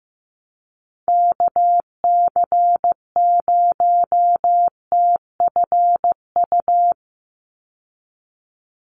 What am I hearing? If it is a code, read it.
Morse code, heard as a keyed tone, about 15 wpm, 700 Hz. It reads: KC0TFU
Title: USA Call Signs 15wpm